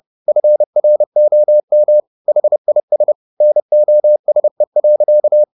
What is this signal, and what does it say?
Morse code, heard as a keyed tone, about 30 wpm, 600 Hz. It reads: FROM HIS NOSE.